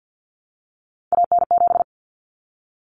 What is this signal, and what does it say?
Morse code, heard as a keyed tone, about 50 wpm, 700 Hz. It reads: AN7
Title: Common Cut Numbers - 3-Digit 50wpm